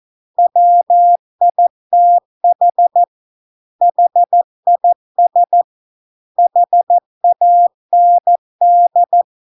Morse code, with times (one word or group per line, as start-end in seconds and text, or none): 0.4-3.0 WITH
3.8-5.6 HIS
6.4-9.2 HAND